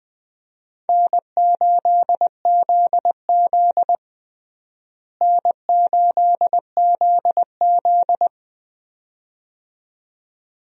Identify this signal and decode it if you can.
Morse code, heard as a keyed tone, about 20 wpm, 700 Hz. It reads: N8ZZ N8ZZ
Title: USA Call Signs - 4 Characters - Repeated 2x - 20wpm